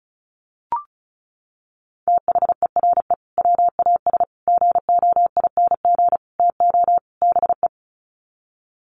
Morse code, heard as a keyed tone, about 34 wpm, 700 Hz. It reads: THERE WAS GOING TO BE
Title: Sets of 5 Words Encore2 35wpm